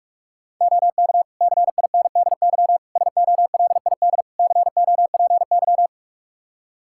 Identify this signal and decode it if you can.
Morse code, heard as a keyed tone, about 46 wpm, 700 Hz. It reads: OK CINDY SOLID COPY
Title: K1USN SST QSO Practice 45wpm